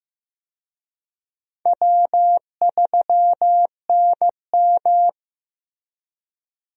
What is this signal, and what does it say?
Morse code, heard as a keyed tone, about 15 wpm, 700 Hz. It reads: W3NM
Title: USA Call Signs - 4 Characters - Repeated 3x - 15wpm